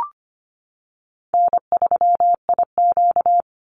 Morse code, heard as a keyed tone, about 25 wpm, 700 Hz. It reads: N3IQ
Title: Call Signs - 25wpm